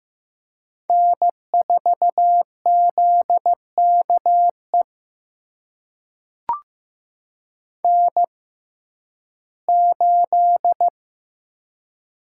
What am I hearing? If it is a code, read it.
Morse code, heard as a keyed tone, about 15 wpm, spaced as for 11 wpm, 700 Hz. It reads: N4ZKE N8
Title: USA Call Signs - 5 Characters - 6x Character Spacing - 15wpm